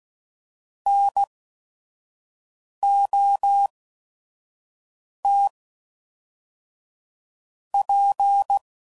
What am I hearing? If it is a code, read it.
Morse code, heard as a keyed tone, about 16 wpm, spaced as for 5 wpm, 800 Hz. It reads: NOT P
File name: CW Code Groups @5WPM